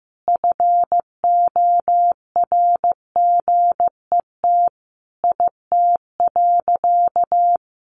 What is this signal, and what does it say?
Morse code, heard as a keyed tone, about 15 wpm, 700 Hz. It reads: FORGET IT.
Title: Sentences from Top 1200 Words 15wpm